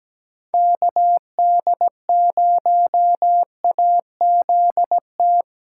Morse code, heard as a keyed tone, about 20 wpm, 700 Hz. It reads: KD0AZT
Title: Speed Racing - USA 2x3 Call Signs 17wpm